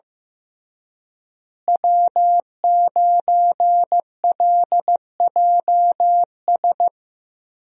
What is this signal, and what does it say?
Morse code, heard as a keyed tone, about 15 wpm, 700 Hz. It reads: W9LJS
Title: USA Call Signs - 5 Characters - Repeated 3x - 15wpm